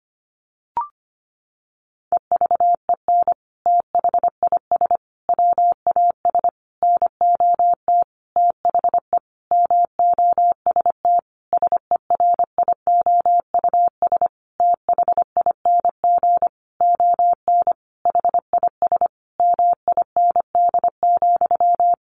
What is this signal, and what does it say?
Morse code, heard as a keyed tone, about 25 wpm, 700 Hz. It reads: EVEN THIS WAS NOT THE MOST SERIOUS THING ON HIS MIND,
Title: Sentences from Top 1100 Words 25wpm